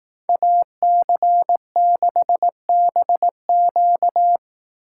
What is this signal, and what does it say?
Morse code, heard as a keyed tone, about 18 wpm, 700 Hz. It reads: AC6BQ
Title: Speed Racing - USA 2x2 Call Signs 15wpm